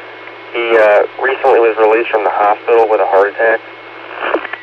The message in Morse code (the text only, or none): none